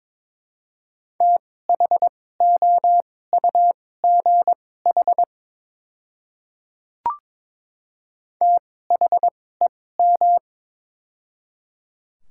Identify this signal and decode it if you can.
Morse code, heard as a keyed tone, about 22 wpm, spaced as for 16 wpm, 700 Hz. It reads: THOUGH THEM
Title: 2x Character Spacing - Top 500 Words - 22wpm